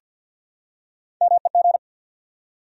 Morse code, heard as a keyed tone, about 49 wpm, 700 Hz. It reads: MEG
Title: USA Common Names - Max 4 Letters - 50wpm